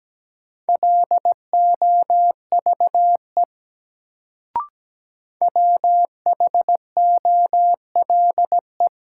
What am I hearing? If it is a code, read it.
Morse code, heard as a keyed tone, about 17 wpm, 700 Hz. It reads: LOVE WHOLE
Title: Sentences from Top 400 Words - New Word Drill - 17wpm